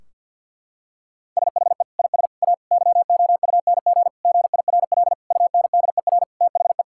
Morse code, heard as a keyed tone, about 50 wpm, 700 Hz. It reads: SHE IS A YOUNG GIRL UNDER THE
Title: Sentences from Top 1600 Words 50wpm